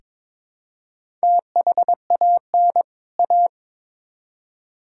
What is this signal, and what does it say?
Morse code, heard as a keyed tone, about 21 wpm, 700 Hz. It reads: THAN A
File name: Build A Phrase - 2-Word Phrases - 22wpm